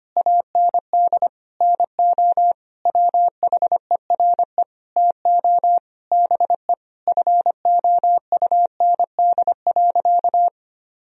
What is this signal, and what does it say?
Morse code, heard as a keyed tone, about 25 wpm, 700 Hz. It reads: AND NO WHERE TO BE FOUND.
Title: Sentences from Top 1000 Words - Review Part 3 - 25wpm